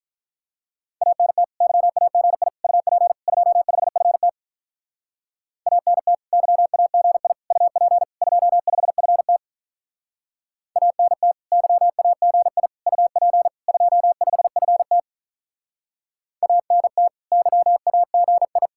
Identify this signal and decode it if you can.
Morse code, heard as a keyed tone, about 44 wpm, 700 Hz. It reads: ANT YAGI UP 25FT ANT YAGI UP 25FT ANT YAGI UP 25FT ANT YAGI
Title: Speed-Racing Common QSO Phrases - 35wpm